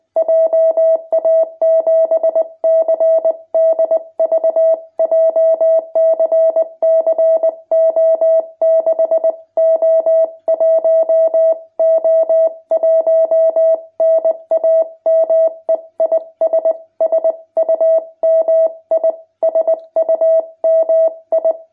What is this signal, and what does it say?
Morse code, heard as a keyed tone, about 20 wpm, 660 Hz. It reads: JA7CDV JCC O6O1O1 NAME IS SUMI SUMI
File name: jh8cbh / 2021cw